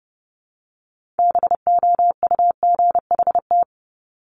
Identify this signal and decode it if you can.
Morse code, heard as a keyed tone, about 30 wpm, 700 Hz. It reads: BOUGHT
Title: Multi-letter Phonemes 30wpm